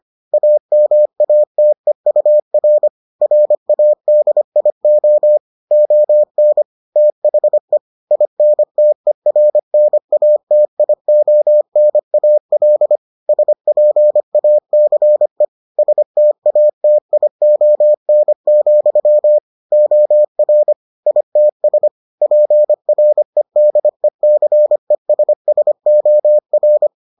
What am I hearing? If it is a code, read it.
Morse code, heard as a keyed tone, about 25 wpm, 600 Hz. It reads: AMATEUR RADIO ON THE INTERNATIONAL SPACE STATION, OR ITS PREDECESSOR